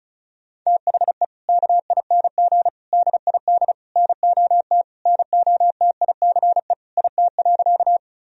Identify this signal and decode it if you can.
Morse code, heard as a keyed tone, about 36 wpm, 700 Hz. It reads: THE KING DID NOT NOTICE IT.
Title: Sentences from Top 1000 Words - Review Part 2 - 35wpm